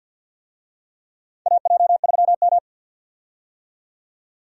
Morse code, heard as a keyed tone, about 49 wpm, 700 Hz. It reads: AJ3M
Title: USA Call Signs - 4 Characters - 50wpm